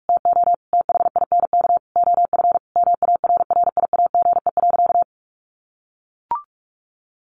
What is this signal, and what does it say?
Morse code, heard as a keyed tone, about 44 wpm, 700 Hz. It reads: TO THINK OF MARRIAGE.
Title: Sentences from Top 800 Words 45wpm